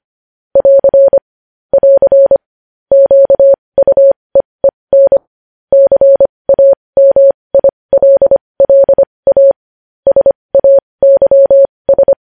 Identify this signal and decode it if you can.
Morse code, heard as a keyed tone, about 25 wpm, spaced as for 20 wpm, 550 Hz. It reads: <AR> <AR> QUEEN CAMILLA SAYS
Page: News Headlines in Morse Code at 20 WPM